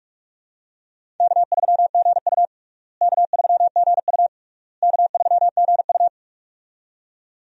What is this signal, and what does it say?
Morse code, heard as a keyed tone, about 42 wpm, 700 Hz. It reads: K3GU K3GU K3GU
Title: USA Call Signs - 4 Characters - Repeated 3x - 45wpm